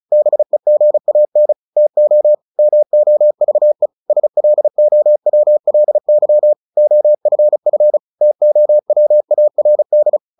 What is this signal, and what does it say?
Morse code, heard as a keyed tone, about 34 wpm, 600 Hz. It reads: BEGAN TO MOVE SLOWLY OFF TOWARD
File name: the old man and the sea / the old man and the sea 1 35wpm